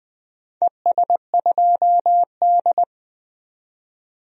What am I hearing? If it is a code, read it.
Morse code, heard as a keyed tone, about 20 wpm, 700 Hz. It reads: ES2D
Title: DXCC-100 Call Signs - 4 Characters - 20wpm